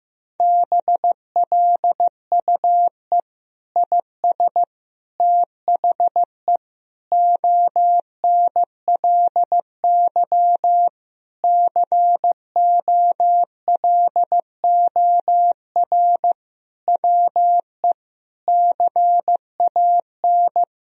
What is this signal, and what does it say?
Morse code, heard as a keyed tone, about 15 wpm, 700 Hz. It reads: BLUE IS THE ONLY COLOR WE CAN
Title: Sentences from Top 1000 Words - Review Part 3 - 15wpm